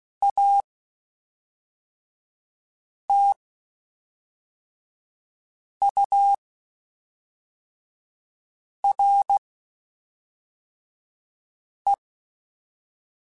Morse code, heard as a keyed tone, about 16 wpm, 800 Hz. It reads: A T U R E
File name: Spaced Characters / Spaced Characters-mp3s / Weather4 14 sp @5